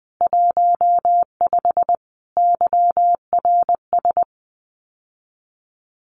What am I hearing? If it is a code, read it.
Morse code, heard as a keyed tone, about 19 wpm, 700 Hz. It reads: 15 YRS
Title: Common QSO Phrases Encore 20wpm